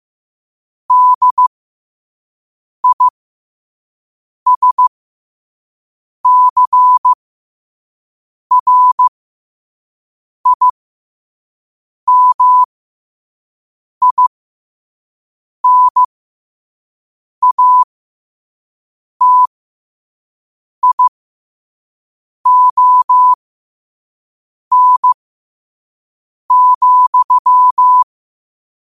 Morse code, heard as a keyed tone, about 15 wpm, spaced as for 5 wpm, 1000 Hz. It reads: DISCRIMINATION,
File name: qotdfiles / 5wpm